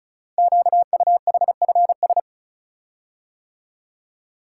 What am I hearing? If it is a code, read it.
Morse code, heard as a keyed tone, about 36 wpm, 700 Hz. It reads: QUHFS
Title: Code Groups - 5 Letters - 35wpm